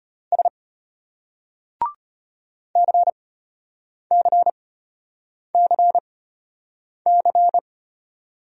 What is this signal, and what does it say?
Morse code, heard as a keyed tone, about 33 wpm, spaced as for 8 wpm, 700 Hz. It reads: S CCCC